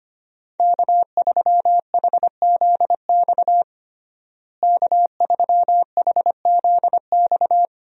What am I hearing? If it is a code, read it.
Morse code, heard as a keyed tone, about 25 wpm, 700 Hz. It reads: K3HZX K3HZX
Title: USA Call Signs - 5 Characters - Repeated 2x - 25wpm